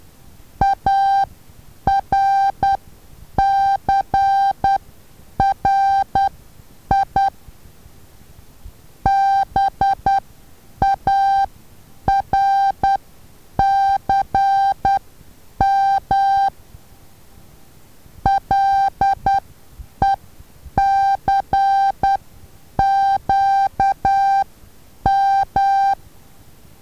Morse code, heard as a keyed tone, about 10 wpm, spaced as for 8 wpm, 790 Hz. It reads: ARCRI BARCM LECQM
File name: Taviro HA6VA B